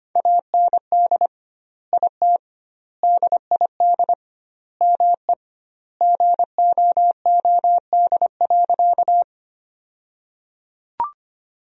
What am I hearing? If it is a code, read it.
Morse code, heard as a keyed tone, about 25 wpm, 700 Hz. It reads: AND IT DID ME GOOD.
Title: 2x Word-Spacing - Sentences from Top 200 Words - 25wpm